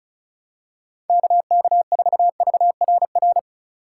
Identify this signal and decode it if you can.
Morse code, heard as a keyed tone, about 36 wpm, 700 Hz. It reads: KK4VRR
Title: USA Call Signs - 6 Characters - 3x Character Spacing - 35wpm